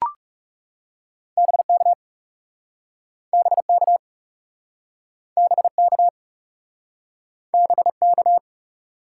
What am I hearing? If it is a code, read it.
Morse code, heard as a keyed tone, about 40 wpm, 700 Hz. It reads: BK BK BK BK